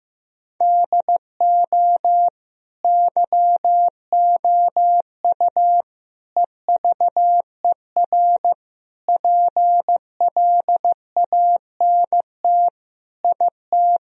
Morse code, heard as a keyed tone, about 16 wpm, 700 Hz. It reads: DO YOU EVER PLANT IT
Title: Sentences from Top 1000 Words - Review Part 1 - 15wpm